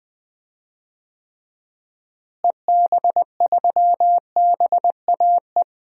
Morse code, heard as a keyed tone, about 20 wpm, 700 Hz. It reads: EB3BAE
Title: DXCC-100 Call Signs - 6 Characters - Repeated 3x - 20wpm